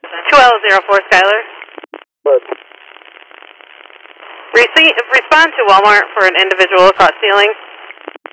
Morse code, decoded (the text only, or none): none